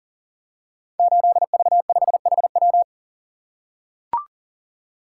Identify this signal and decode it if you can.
Morse code, heard as a keyed tone, about 38 wpm, 700 Hz. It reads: 8V5HW
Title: Code Groups - 5 Letter:Numbers - 40wpm